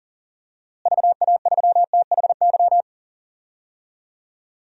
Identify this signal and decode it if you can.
Morse code, heard as a keyed tone, about 37 wpm, 700 Hz. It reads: VA3THY
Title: DXCC-100 Call Signs - 6 Characters - Repeated 3x - 40wpm